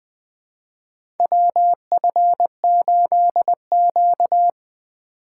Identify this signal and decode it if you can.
Morse code, heard as a keyed tone, about 20 wpm, 700 Hz. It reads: WF8Q
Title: USA Call Signs - 4 Characters - 3x Character Spacing - 20wpm